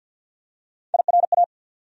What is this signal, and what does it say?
Morse code, heard as a keyed tone, about 48 wpm, 700 Hz. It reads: IRA
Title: USA Call Sign Postfixes - 3 Letters 50wpm